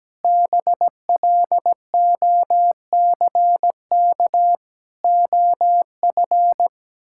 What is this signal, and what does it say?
Morse code, heard as a keyed tone, about 17 wpm, 700 Hz. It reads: BLOCK OF
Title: Sentences from Top 1600 Words 17wpm